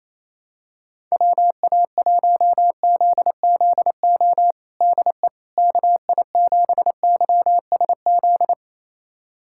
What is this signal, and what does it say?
Morse code, heard as a keyed tone, about 28 wpm, 700 Hz. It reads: WA1ZZO DE KI7YSZ
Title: USA Callsign DE Callsign - 6 Characters - 28wpm